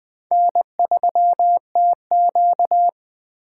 Morse code, heard as a keyed tone, about 20 wpm, 700 Hz. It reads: N3TQ